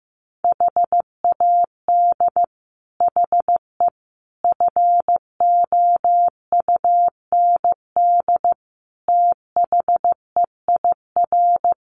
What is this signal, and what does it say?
Morse code, heard as a keyed tone, about 15 wpm, 700 Hz. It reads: HAD HE FOUND THEIR
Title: Sentences from Top 1300 Words 15wpm